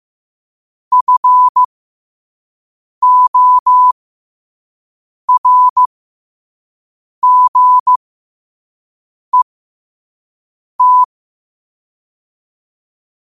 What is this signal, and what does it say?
Morse code, heard as a keyed tone, about 15 wpm, spaced as for 5 wpm, 1000 Hz. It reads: FORGET
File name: qotdfiles / 5wpm